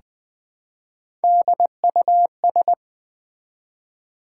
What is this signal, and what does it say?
Morse code, heard as a keyed tone, about 20 wpm, 700 Hz. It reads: DUS
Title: Build A Word - 1 Letter At A Time - 4-Letter Words - 20wpm